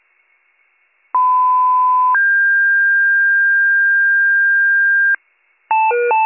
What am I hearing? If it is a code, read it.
Two-tone page, heard as a one-second tone, then a three-second tone, about 1010 then 1600 Hz. It is heard from 1.1 to 5.2 s.